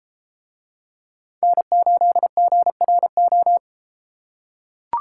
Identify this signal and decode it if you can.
Morse code, heard as a keyed tone, about 32 wpm, 700 Hz. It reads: N8GRO